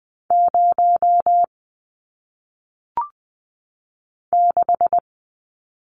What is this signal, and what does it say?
Morse code, heard as a keyed tone, about 20 wpm, 700 Hz. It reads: 0 6
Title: Numbers - 1-Digit 20wpm